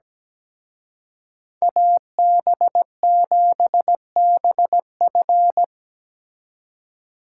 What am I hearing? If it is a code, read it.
Morse code, heard as a keyed tone, about 20 wpm, 700 Hz. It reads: AB7BF
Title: Speed Racing - USA 2x2 Call Signs 17wpm